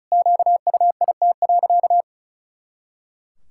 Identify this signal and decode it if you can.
Morse code, heard as a keyed tone, about 34 wpm, 700 Hz. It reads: QUIT.
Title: Pangram Challenge v2 - 35wpm